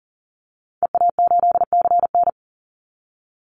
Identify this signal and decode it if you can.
Morse code, heard as a keyed tone, about 40 wpm, 700 Hz. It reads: EA8CN
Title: Call Signs Repeated 2x - 40wpm